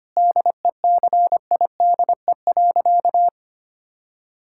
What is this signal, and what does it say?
Morse code, heard as a keyed tone, about 25 wpm, 700 Hz. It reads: DECIDE.
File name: practice / 25wpm / Sentences with 2-Word Phrases - 25wpm